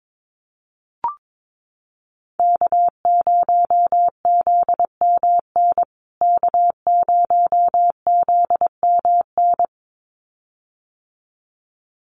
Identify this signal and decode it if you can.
Morse code, heard as a keyed tone, about 22 wpm, 700 Hz. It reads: K0ZMN K0ZMN
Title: USA Call Signs Repeated 2x 22wpm